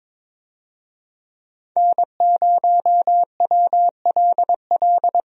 Morse code, heard as a keyed tone, about 22 wpm, 700 Hz. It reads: N0WLL
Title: USA Call Signs - 5 Characters - 22wpm